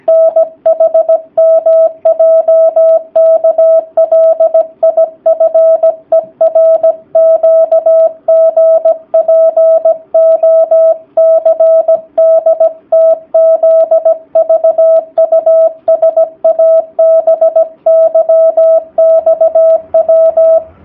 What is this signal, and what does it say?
Morse code, heard as a keyed tone, about 17 wpm, 650 Hz. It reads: NHMJKLIFERQGPOCDTZVUSABYXW
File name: jh8cbh / 2019cw / sp16- w